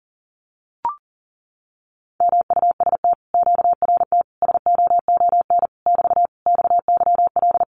Morse code, heard as a keyed tone, about 40 wpm, 700 Hz. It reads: MUST QRT SOON <BT> XYL
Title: Common QSO Phrases Encore 40wpm